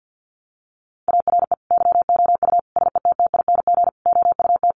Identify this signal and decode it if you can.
Morse code, heard as a keyed tone, about 50 wpm, 700 Hz. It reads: ARE YOU SETTING OUT
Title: Sentences from Top 1300 Words 50wpm